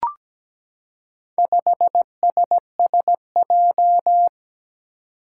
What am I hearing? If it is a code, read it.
Morse code, heard as a keyed tone, about 17 wpm, 700 Hz. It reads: 5SSJ
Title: Four Challenging Letter-Number Combinations 17wpm